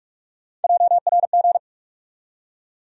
Morse code, heard as a keyed tone, about 45 wpm, 700 Hz. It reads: JRG
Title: USA Call Sign Postfixes - 3 Letters 45wpm